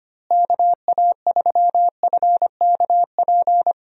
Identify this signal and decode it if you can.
Morse code, heard as a keyed tone, about 25 wpm, 700 Hz. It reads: KA3FKP
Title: USA Call Signs - 6 Characters - Repeated 3x - 25wpm